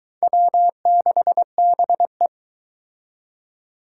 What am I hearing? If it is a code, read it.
Morse code, heard as a keyed tone, about 23 wpm, 700 Hz. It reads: W6BE